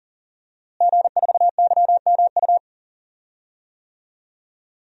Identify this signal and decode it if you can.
Morse code, heard as a keyed tone, about 40 wpm, 700 Hz. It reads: G4YMU
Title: DXCC-100 Call Signs - 5 Characters - Repeated 2x - 40wpm